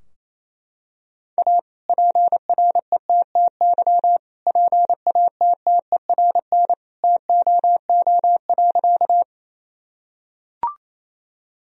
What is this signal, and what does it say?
Morse code, heard as a keyed tone, about 28 wpm, 700 Hz. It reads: A PRETTY PATTERN TOO.
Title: Sentences from Top 1000 Words - Review Part 2 - 28wpm